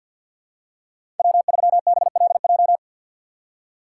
Morse code, heard as a keyed tone, about 47 wpm, 700 Hz. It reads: W3BLJ
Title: USA Call Signs - 3x Character Spacing - 50wpm